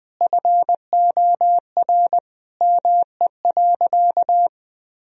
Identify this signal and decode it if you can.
Morse code, heard as a keyed tone, about 20 wpm, 700 Hz. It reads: FOR ME.
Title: Sentences from Top 1000 Words - Review Part 4 - 20wpm